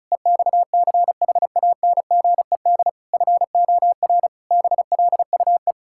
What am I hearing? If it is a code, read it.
Morse code, heard as a keyed tone, about 34 wpm, 700 Hz. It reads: EXCHANGED FOR BLUE
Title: Pangram Challenge v2 - 35wpm